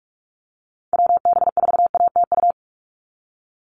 Morse code, heard as a keyed tone, about 46 wpm, 690 Hz. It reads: WB4ATU
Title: USA Call Signs 45wpm